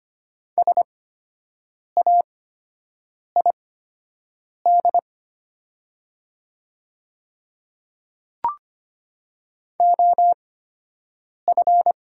Morse code, heard as a keyed tone, about 25 wpm, spaced as for 7 wpm, 700 Hz. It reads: SAID OF
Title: 8x Character Spacing - Top 100 Words - 25wpm